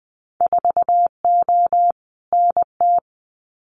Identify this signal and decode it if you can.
Morse code, heard as a keyed tone, about 20 wpm, 700 Hz. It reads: 4O NT
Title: Winter Field Day Exchanges Encore 20wpm